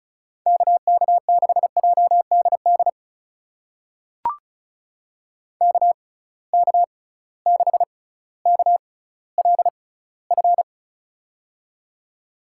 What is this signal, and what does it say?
Morse code, heard as a keyed tone, about 36 wpm, spaced as for 18 wpm, 700 Hz. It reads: KK6JDD KK6KLF
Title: USA Call Signs - 6 Characters - 6x Character Spacing - 35wpm